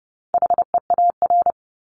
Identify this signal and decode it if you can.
Morse code, heard as a keyed tone, about 30 wpm, 700 Hz. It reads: HEAR
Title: Top 300 Words Repeated 30wpm